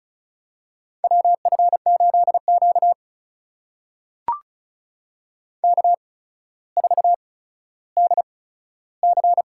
Morse code, heard as a keyed tone, about 36 wpm, spaced as for 15 wpm, 700 Hz. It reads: WF8Q K4DC
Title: USA Call Signs - 4 Characters - 8x Character Spacing - 35wpm